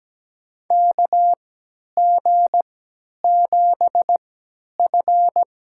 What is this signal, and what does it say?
Morse code, heard as a keyed tone, about 17 wpm, spaced as for 10 wpm, 700 Hz. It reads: KG7F
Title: USA Call Signs - 3x Character Spacing - 17wpm